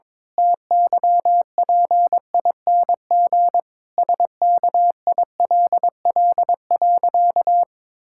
Morse code, heard as a keyed tone, about 22 wpm, 700 Hz. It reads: TYPING SKILL.
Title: Pangram Challenge v2 - 22wpm